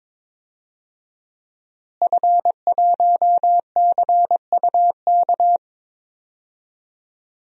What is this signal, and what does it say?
Morse code, heard as a keyed tone, about 21 wpm, 700 Hz. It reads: F1CUK
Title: DXCC-100 Call Signs - 5 Characters - Repeated 3x - 22wpm